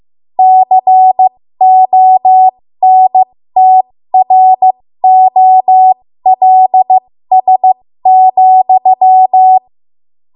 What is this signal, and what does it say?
Morse code, heard as a keyed tone, about 15 wpm, spaced as for 13 wpm, 750 Hz. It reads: CONTROLS,
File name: Morse / 13WPM